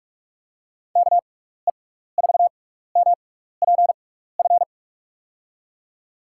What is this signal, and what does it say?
Morse code, heard as a keyed tone, about 44 wpm, spaced as for 15 wpm, 700 Hz. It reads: KE4MPF
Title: USA Call Signs - 6x Character Spacing - 45wpm